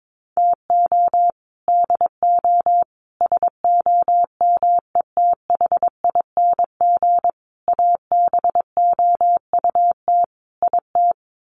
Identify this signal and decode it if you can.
Morse code, heard as a keyed tone, about 22 wpm, 700 Hz. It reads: TO DO SOMETHING ABOUT IT
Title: Sets of 5 Words 22wpm